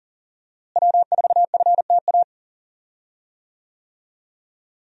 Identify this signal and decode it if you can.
Morse code, heard as a keyed tone, about 37 wpm, 700 Hz. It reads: W4FTA
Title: USA Call Signs - 5 Characters - Repeated 2x - 40wpm